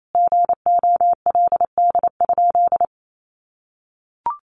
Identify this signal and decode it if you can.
Morse code, heard as a keyed tone, about 28 wpm, 700 Hz. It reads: GOLD?
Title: Sentences from Top 1500 Words 28wpm